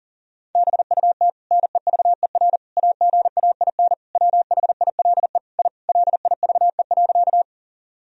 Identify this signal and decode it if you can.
Morse code, heard as a keyed tone, about 40 wpm, 700 Hz. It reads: BUT NEVER AGAIN WHILE I LIVE.
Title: Sentences from Top 1000 Words - Review Part 1 - 40wpm